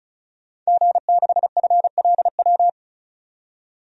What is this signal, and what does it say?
Morse code, heard as a keyed tone, about 34 wpm, 700 Hz. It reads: G6FLW
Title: DXCC-100 Call Signs - 5 Characters - 35wpm